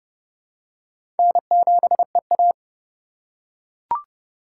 Morse code, heard as a keyed tone, about 31 wpm, 700 Hz. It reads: N7EA
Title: USA Call Signs - 4 Characters - 2x Character Spacing - 30wpm